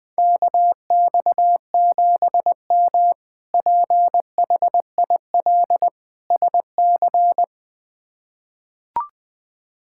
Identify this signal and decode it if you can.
Morse code, heard as a keyed tone, about 20 wpm, 700 Hz. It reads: KX7M PHIL SC
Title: K1USN SST QSO Practice - 20wpm